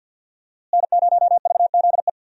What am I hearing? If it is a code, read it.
Morse code, heard as a keyed tone, about 47 wpm, 700 Hz. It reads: N0VZE
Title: USA Call Signs - 5 Characters - Repeated 3x - 50wpm